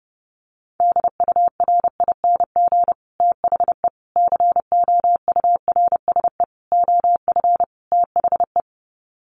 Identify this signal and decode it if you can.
Morse code, heard as a keyed tone, about 30 wpm, 700 Hz. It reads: DURING THE COURSE OF THE
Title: Sets of 5 Words Encore 30wpm